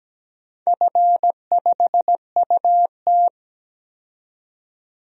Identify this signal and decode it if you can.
Morse code, heard as a keyed tone, about 17 wpm, 700 Hz. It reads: F5UT
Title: DXCC-100 Call Signs - 4 Characters - 17wpm